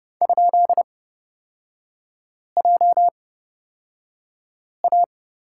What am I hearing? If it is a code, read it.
Morse code, heard as a keyed tone, about 30 wpm, spaced as for 5 wpm, 700 Hz. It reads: ?JA
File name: Lesson 36e - BT - Three Characters 30wpm